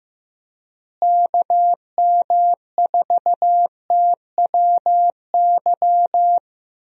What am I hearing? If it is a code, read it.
Morse code, heard as a keyed tone, about 16 wpm, 700 Hz. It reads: KM4TWY
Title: USA Call Signs - 6 Characters - Repeated 2x - 15wpm